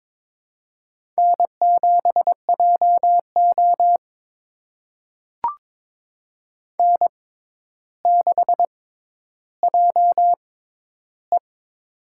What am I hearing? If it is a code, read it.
Morse code, heard as a keyed tone, about 22 wpm, spaced as for 11 wpm, 700 Hz. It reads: N7JO N6JE
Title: USA Call Signs - 4 Characters - 6x Character Spacing - 22wpm